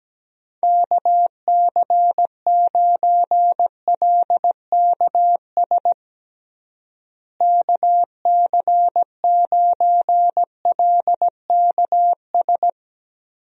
Morse code, heard as a keyed tone, about 17 wpm, 700 Hz. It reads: KC9LKS KC9LKS
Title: USA Call Signs - 6 Characters - Repeated 3x - 17wpm